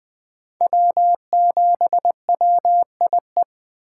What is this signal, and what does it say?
Morse code, heard as a keyed tone, about 20 wpm, 700 Hz. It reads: W7WIE
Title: USA Call Signs - 5 Characters - 4x Character Spacing - 20wpm